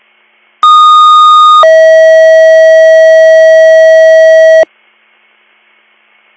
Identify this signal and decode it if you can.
Two-tone page, heard as a one-second tone, then a three-second tone, about 1230 then 640 Hz. It is heard from 0.6 to 4.6 s.